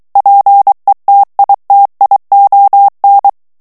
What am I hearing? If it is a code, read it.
Morse code, heard as a keyed tone, about 23 wpm, 800 Hz. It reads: PETITION
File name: ANGLAIS texte clair 3 vitesse 22 mots minute